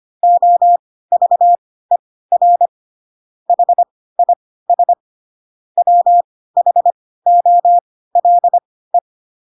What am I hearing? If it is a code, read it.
Morse code, heard as a keyed tone, about 25 wpm, spaced as for 16 wpm, 700 Hz. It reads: OVER HIS WHOLE
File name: Pinocchio / Pinocchio ch6 10 25 ik1hge